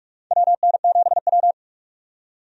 Morse code, heard as a keyed tone, about 44 wpm, 700 Hz. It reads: WN7W
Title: USA Call Signs - 4 Characters - 45wpm